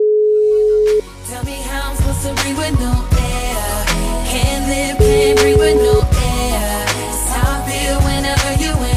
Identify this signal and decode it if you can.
Ringback tone (Europe, 425 Hz), with noise.